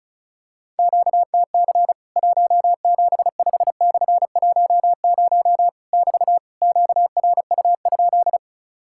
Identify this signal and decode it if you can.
Morse code, heard as a keyed tone, about 36 wpm, 700 Hz. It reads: QTC 175/10 <BT> QRU?
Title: Worked All Europe DX Contest - QTC Exchange - 35wpm